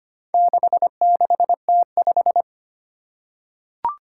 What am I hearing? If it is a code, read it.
Morse code, heard as a keyed tone, about 25 wpm, 700 Hz. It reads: 66T5
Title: Four Challenging Letter-Number Combinations 25wpm